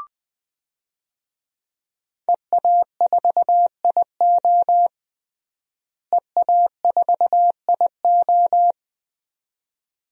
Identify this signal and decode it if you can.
Morse code, heard as a keyed tone, about 20 wpm, 700 Hz. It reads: EA4IO EA4IO
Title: DXCC-100 Call Signs - 5 Characters - Repeated 2x - 20wpm